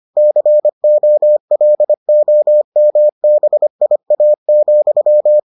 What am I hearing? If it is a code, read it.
Morse code, heard as a keyed tone, about 25 wpm, 600 Hz. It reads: COLOMBIA,